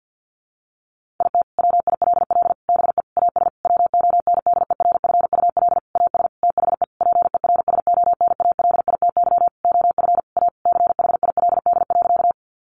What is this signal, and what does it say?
Morse code, heard as a keyed tone, about 50 wpm, 700 Hz. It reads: IT WILL BE AS WONDERFUL AS THE PERSONALITY OF A CHILD.
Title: Sentences from Top 1400 Words 50wpm